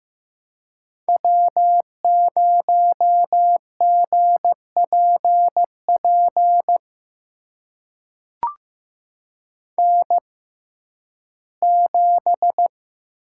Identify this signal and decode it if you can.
Morse code, heard as a keyed tone, about 15 wpm, spaced as for 11 wpm, 700 Hz. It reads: W0GPP N7
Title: USA Call Signs - 5 Characters - 6x Character Spacing - 15wpm